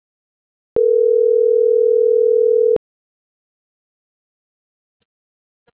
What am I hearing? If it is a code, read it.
Ringback tone (North America), 440 + 480 Hz.